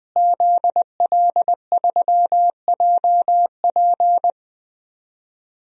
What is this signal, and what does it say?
Morse code, heard as a keyed tone, about 20 wpm, 700 Hz. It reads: ZL3JP
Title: DXCC-100 Call Signs - 5 Characters - Repeated 2x - 20wpm